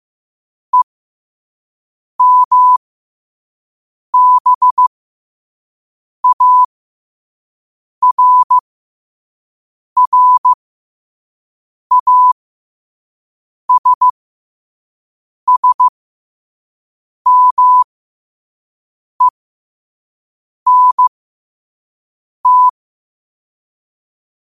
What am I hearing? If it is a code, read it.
Morse code, heard as a keyed tone, about 15 wpm, spaced as for 5 wpm, 1000 Hz. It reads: EMBARRASSMENT